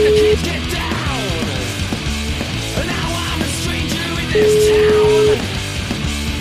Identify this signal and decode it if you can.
Ringback tone (Europe, 425 Hz), with speech.